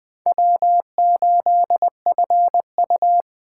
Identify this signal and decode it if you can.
Morse code, heard as a keyed tone, about 21 wpm, 700 Hz. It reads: W8FU